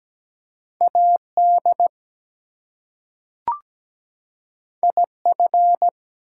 Morse code, heard as a keyed tone, about 17 wpm, 700 Hz. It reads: AD IF
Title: Two Letter Words - 17wpm